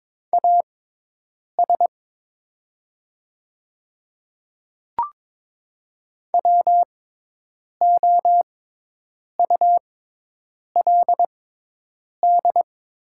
Morse code, heard as a keyed tone, about 22 wpm, spaced as for 8 wpm, 690 Hz. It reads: AS WOULD
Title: 6x Character Spacing - Top 100 Words - 22wpm